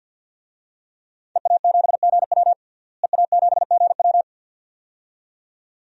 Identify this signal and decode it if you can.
Morse code, heard as a keyed tone, about 46 wpm, 700 Hz. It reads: EA7GW EA7GW
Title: DXCC-100 Call Signs - 5 Characters - Repeated 2x - 50wpm